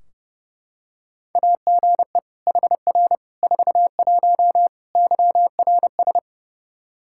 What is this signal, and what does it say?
Morse code, heard as a keyed tone, about 31 wpm, 700 Hz. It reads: AGE HR 41 YRS
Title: Practice QSO - Exchange 3 - More Info - 30wpm